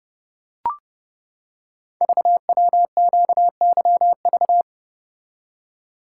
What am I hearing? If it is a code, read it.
Morse code, heard as a keyed tone, about 30 wpm, 700 Hz. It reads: VWQYV
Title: Code Groups - 5 Letters - 30wpm